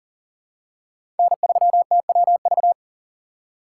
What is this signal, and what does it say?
Morse code, heard as a keyed tone, about 40 wpm, 700 Hz. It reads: N3TWV
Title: USA Call Signs - 5 Characters - 4x Character Spacing - 40wpm